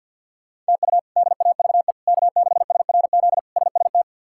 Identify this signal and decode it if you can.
Morse code, heard as a keyed tone, about 51 wpm, 700 Hz. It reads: TU DAVE K6SRZ SST
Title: K1USN SST QSO Practice - 50wpm